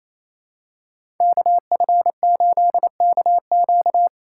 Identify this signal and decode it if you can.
Morse code, heard as a keyed tone, about 28 wpm, 700 Hz. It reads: KF8KQ
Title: USA Call Signs - 5 Characters - 4x Character Spacing - 28wpm